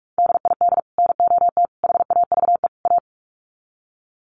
Morse code, heard as a keyed tone, about 44 wpm, 700 Hz. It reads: DID NOT HAVE A
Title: Sets of 4 Words 45wpm